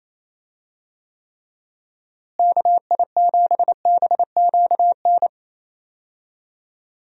Morse code, heard as a keyed tone, about 28 wpm, 700 Hz. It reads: KI7BQN
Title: USA Call Signs - 6 Characters - Repeated 3x - 28wpm